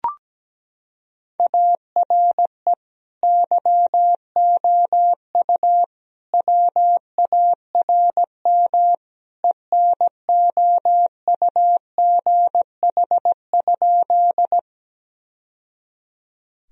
Morse code, heard as a keyed tone, about 17 wpm, 700 Hz. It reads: ARE YOU WARM ENOUGH?
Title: Sentences from Top 1000 Words - Review Part 2 - 17wpm